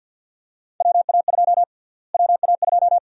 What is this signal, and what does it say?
Morse code, heard as a keyed tone, about 52 wpm, 700 Hz. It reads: WA2 WA2
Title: USA Call Sign Prefixes - Repeated 2x - 50wpm